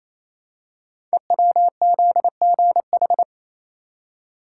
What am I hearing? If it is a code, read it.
Morse code, heard as a keyed tone, about 27 wpm, 700 Hz. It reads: EWZGH
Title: Code Groups - 5 Letters - 28wpm